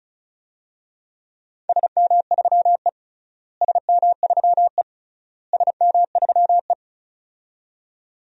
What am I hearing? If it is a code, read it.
Morse code, heard as a keyed tone, about 34 wpm, 700 Hz. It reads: SM3E SM3E SM3E
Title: DXCC-100 Call Signs - 4 Characters - Repeated 3x - 35wpm